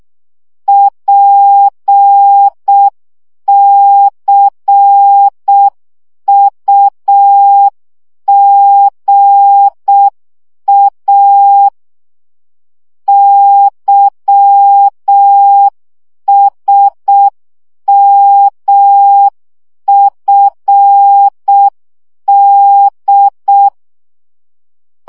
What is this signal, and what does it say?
Morse code, heard as a keyed tone, about 6 wpm, 800 Hz. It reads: PCUGA YSMFD
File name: morze vizsga 2011-2 5